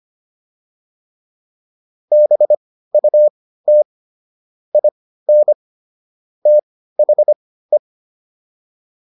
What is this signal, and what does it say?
Morse code, heard as a keyed tone, about 25 wpm, spaced as for 15 wpm, 600 Hz. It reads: BUT IN THE
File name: WD205 15